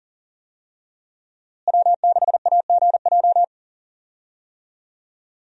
Morse code, heard as a keyed tone, about 40 wpm, 700 Hz. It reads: W6AGJ
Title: USA Call Signs - 5 Characters - Repeated 3x - 40wpm